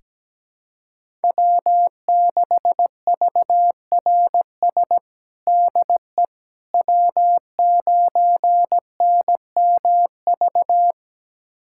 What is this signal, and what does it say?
Morse code, heard as a keyed tone, about 17 wpm, 700 Hz. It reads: W6VRS DE W9NMV
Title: USA Callsign DE Callsign - 5 Characters - 17wpm